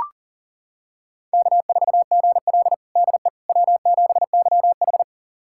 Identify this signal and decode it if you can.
Morse code, heard as a keyed tone, about 37 wpm, 700 Hz. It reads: K4GL DE W7YH
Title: USA Callsign DE Callsign - 4 Characters - 40wpm